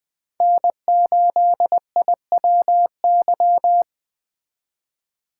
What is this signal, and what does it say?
Morse code, heard as a keyed tone, about 20 wpm, 700 Hz. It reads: N8IWY
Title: K1USN SST QSO Practice - 20wpm